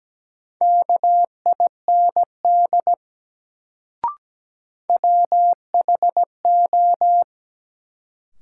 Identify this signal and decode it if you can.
Morse code, heard as a keyed tone, about 17 wpm, 700 Hz. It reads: KIND WHO
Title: Sets of 2 Words - New Word Drill - 17wpm